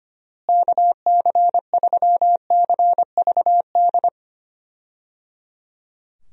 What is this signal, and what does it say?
Morse code, heard as a keyed tone, about 25 wpm, 700 Hz. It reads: KC3CVD
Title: USA Call Signs - 6 Characters - Repeated 3x - 25wpm